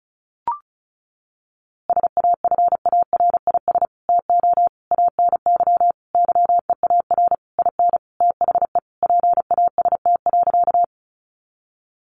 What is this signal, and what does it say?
Morse code, heard as a keyed tone, about 34 wpm, 700 Hz. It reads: SAFARIS TO ANY YEAR IN THE PAST.
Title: A Sound Of Thunder 35wpm